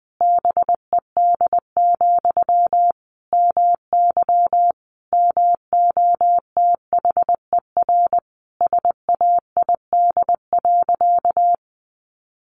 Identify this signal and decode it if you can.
Morse code, heard as a keyed tone, about 20 wpm, 700 Hz. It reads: BED, MY MOTHER SAID.
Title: Sentences from Top 700 Words 20wpm